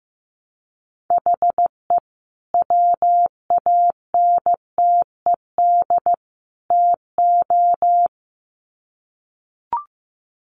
Morse code, heard as a keyed tone, about 15 wpm, 700 Hz. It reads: HE WANTED TO
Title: Sets of 3 Words 15wpm